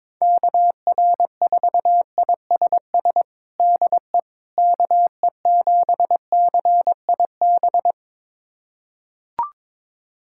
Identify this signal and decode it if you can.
Morse code, heard as a keyed tone, about 22 wpm, 700 Hz. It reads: KR4ISS DE KE7CIB
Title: USA Callsign DE Callsign - 6 Characters - 22wpm